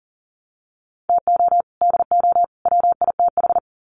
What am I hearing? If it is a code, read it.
Morse code, heard as a keyed tone, about 40 wpm, 700 Hz. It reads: TO DO WITH
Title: Sets of 3 Words 40wpm